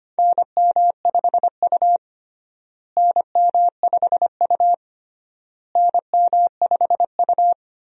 Morse code, heard as a keyed tone, about 25 wpm, 700 Hz. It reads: NM5U NM5U NM5U
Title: USA Call Signs - 4 Characters - Repeated 3x - 25wpm